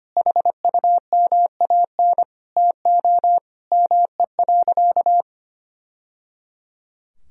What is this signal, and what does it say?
Morse code, heard as a keyed tone, about 25 wpm, 700 Hz. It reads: HUMAN TO ME.
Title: Sentences from Top 1000 Words - Review Part 3 - 25wpm